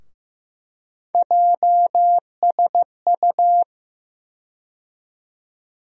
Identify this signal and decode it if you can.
Morse code, heard as a keyed tone, about 15 wpm, 700 Hz. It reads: JSU
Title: USA Call Sign Postfixes - 3 Letters 15wpm